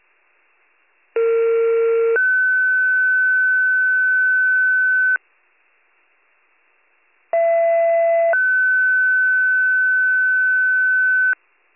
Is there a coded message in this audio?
Yes: two-tone page from 1.2 to 11.3 s, heard as a one-second tone, then a three-second tone, about 460 then 1500 Hz, about 670 then 1500 Hz.